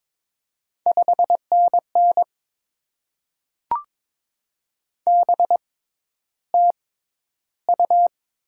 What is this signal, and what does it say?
Morse code, heard as a keyed tone, about 22 wpm, spaced as for 11 wpm, 690 Hz. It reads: 5NN BTU
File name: Top 20 QSO Elements - 6x Character Spacing - 22wpm